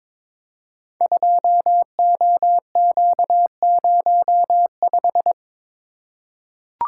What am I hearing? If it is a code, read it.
Morse code, heard as a keyed tone, about 22 wpm, 700 Hz. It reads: 2OQ05
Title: Code Groups - 5 Letter:Numbers - 22wpm